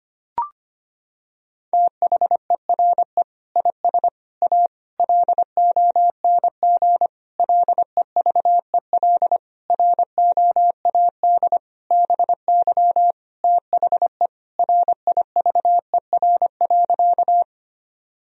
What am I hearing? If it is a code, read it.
Morse code, heard as a keyed tone, about 25 wpm, 700 Hz. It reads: THERE IS A LONG LEVEL ROAD BY THE RIVER.
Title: Sentences from Top 1000 Words - Review Part 4 - 25wpm